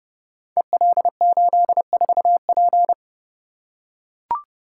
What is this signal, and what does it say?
Morse code, heard as a keyed tone, about 30 wpm, 700 Hz. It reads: EL84P
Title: Code Groups - 5 Letter:Numbers - 30wpm